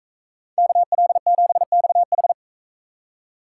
Morse code, heard as a keyed tone, about 40 wpm, 700 Hz. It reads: KL7XH